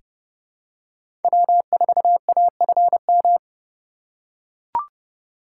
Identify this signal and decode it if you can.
Morse code, heard as a keyed tone, about 30 wpm, 700 Hz. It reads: W4AFM
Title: USA Call Signs - 5 Characters - 3x Character Spacing - 30wpm